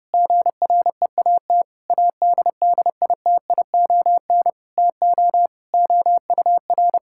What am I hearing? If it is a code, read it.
Morse code, heard as a keyed tone, about 30 wpm, 700 Hz. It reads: GREAT ADDITION TO OUR
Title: Sentences from Top 1000 Words - Review Part 6 - 30wpm